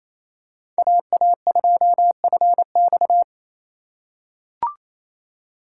Morse code, heard as a keyed tone, about 28 wpm, 700 Hz. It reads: AA2FX
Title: USA Call Signs - 5 Characters - 2x Character Spacing - 28wpm